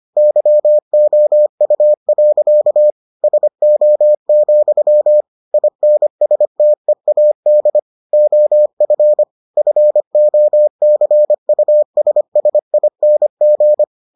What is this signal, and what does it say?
Morse code, heard as a keyed tone, about 25 wpm, 600 Hz. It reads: YOU. SO, INSTEAD OF FOCUSSING